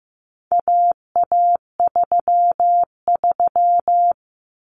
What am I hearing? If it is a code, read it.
Morse code, heard as a keyed tone, about 15 wpm, 700 Hz. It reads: AA33
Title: Common Cut Numbers - 4-Digit 15wpm